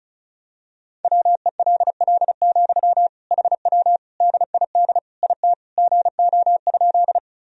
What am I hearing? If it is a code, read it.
Morse code, heard as a keyed tone, about 34 wpm, 700 Hz. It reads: WELL, HW DID IT GO?
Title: Sentences from Top 200 Words and Abbreviations 35wpm